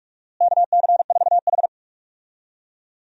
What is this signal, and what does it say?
Morse code, heard as a keyed tone, about 46 wpm, 700 Hz. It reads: KC4H
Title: USA Call Signs - 4 Characters - Repeated 2x - 45wpm